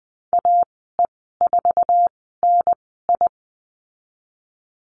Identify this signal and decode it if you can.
Morse code, heard as a keyed tone, about 20 wpm, 700 Hz. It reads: A E 4 N I
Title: 055 Lesson 10g - Letter r - Call Signs 20wpm